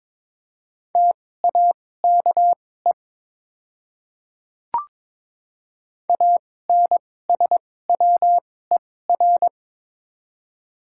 Morse code, heard as a keyed tone, about 22 wpm, spaced as for 16 wpm, 700 Hz. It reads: TAKE ANSWER
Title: 2x Character Spacing - Top 500 Words - 22wpm